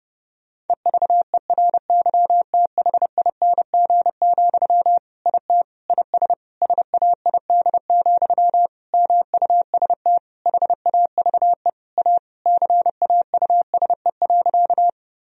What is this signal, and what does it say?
Morse code, heard as a keyed tone, about 32 wpm, 700 Hz. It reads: EVERYTHING, IT IS SAID, MUST HAVE A CAUSE.
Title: Sentences from Top 1000 Words - Review Part 1 - 30wpm